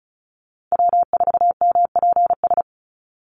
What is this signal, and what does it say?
Morse code, heard as a keyed tone, about 36 wpm, 700 Hz. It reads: W4MPS
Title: NA SOTA Chaser Callsigns 35wpm